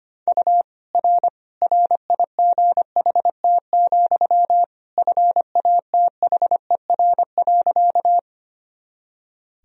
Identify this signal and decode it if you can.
Morse code, heard as a keyed tone, about 25 wpm, 700 Hz. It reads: U R RIGHT, FATHER.
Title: Sentences from Top 300 Words and Abbreviations 25wpm